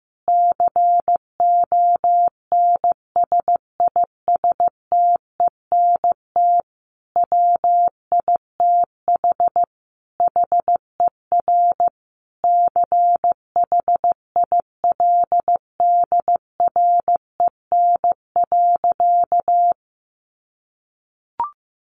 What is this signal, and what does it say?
Morse code, heard as a keyed tone, about 14 wpm, 700 Hz. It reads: CONSISTENT WITH HER CHILDREN.
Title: Sentences from Top 1500 Words 15wpm